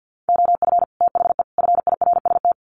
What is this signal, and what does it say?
Morse code, heard as a keyed tone, about 49 wpm, 700 Hz. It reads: OF THE FIRST
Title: Sets of 3 Words 50wpm